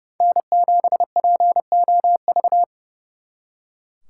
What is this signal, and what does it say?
Morse code, heard as a keyed tone, about 35 wpm, 700 Hz. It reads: N7POV
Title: Speed Racing - USA Call Signs 30wpm